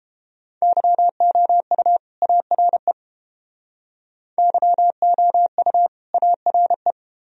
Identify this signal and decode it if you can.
Morse code, heard as a keyed tone, about 31 wpm, 700 Hz. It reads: YOU ARE YOU ARE